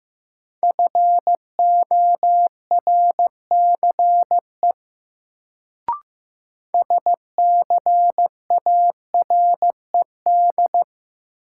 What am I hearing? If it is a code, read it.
Morse code, heard as a keyed tone, about 15 wpm, 700 Hz. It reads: FORCE SCARED
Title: Sentences from Top 700 Words - New Word Drill - 15wpm